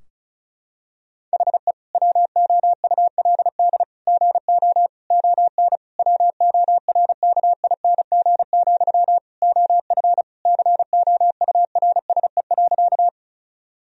Morse code, heard as a keyed tone, about 36 wpm, 700 Hz. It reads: HE WOULD GO ON WORKING, OF COURSE.
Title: Sentences from Top 1000 Words - Review Part 4 - 35wpm